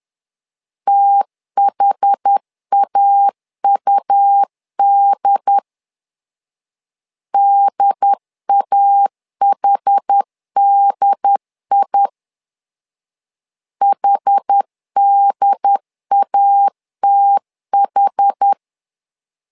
Morse code, heard as a keyed tone, about 10 wpm, 800 Hz. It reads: THAUD DAHDI HDATH
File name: LEZIONI / 2 UHDN / 2 UHD 10wpm 1826